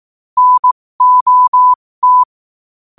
Morse code, heard as a keyed tone, about 18 wpm, spaced as for 15 wpm, 1000 Hz. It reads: NOT
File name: qotdfiles / 15wpm